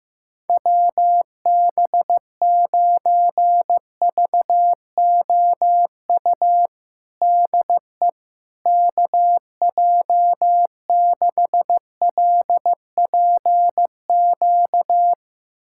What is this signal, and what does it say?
Morse code, heard as a keyed tone, about 15 wpm, 700 Hz. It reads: WB9VOU DE KJ6LPQ
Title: USA Callsign DE Callsign - 6 Characters - 15wpm